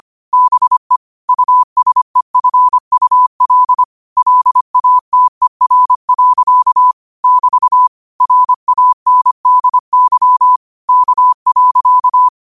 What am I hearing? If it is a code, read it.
Morse code, heard as a keyed tone, about 25 wpm, 1000 Hz. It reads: BE USEFUL LATER. <BT> RANDY K.
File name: qotdfiles / 25wpm